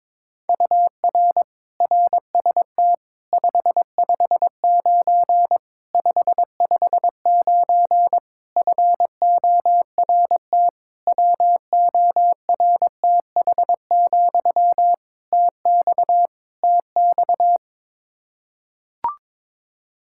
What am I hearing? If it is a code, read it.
Morse code, heard as a keyed tone, about 21 wpm, 700 Hz. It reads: UR RST 559 559 FORT WORTH, TX TX
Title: Practice QSO - Exchange 1 - Beginning - 22wpm